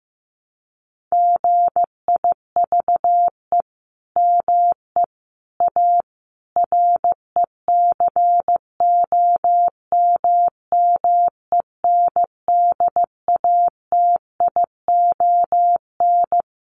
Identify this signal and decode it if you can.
Morse code, heard as a keyed tone, about 15 wpm, 700 Hz. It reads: GIVE ME A RECOMMENDATION
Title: Sentences from Top 1400 Words 15wpm